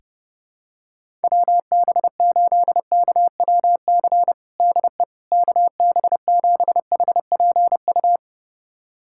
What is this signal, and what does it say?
Morse code, heard as a keyed tone, about 30 wpm, 700 Hz. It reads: WB8KWC DE KB7HPU
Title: USA Callsign DE Callsign - 6 Characters - 30wpm